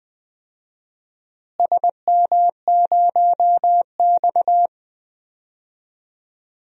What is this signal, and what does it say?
Morse code, heard as a keyed tone, about 20 wpm, 700 Hz. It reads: SM0X
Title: DXCC-100 Call Signs - 4 Characters - Repeated 2x - 20wpm